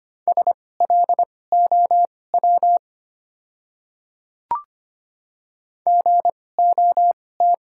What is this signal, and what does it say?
Morse code, heard as a keyed tone, about 25 wpm, spaced as for 18 wpm, 690 Hz. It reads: SLOW GOT
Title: 2x Character Spacing - Top 500 Words - 25wpm